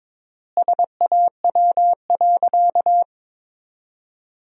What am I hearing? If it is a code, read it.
Morse code, heard as a keyed tone, about 22 wpm, 700 Hz. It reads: SAW.
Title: 3x Word-Spacing - Sentences from Top 300 Words - 22wpm